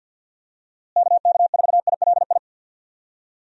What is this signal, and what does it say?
Morse code, heard as a keyed tone, about 52 wpm, 700 Hz. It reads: KK4ILI
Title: USA Call Signs - 6 Characters - Repeated 3x - 50wpm